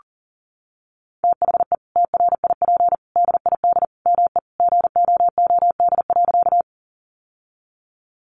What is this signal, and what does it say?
Morse code, heard as a keyed tone, about 40 wpm, 700 Hz. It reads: THE TRIP DID ME GOOD.
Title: Sentences from Top 1300 Words 40wpm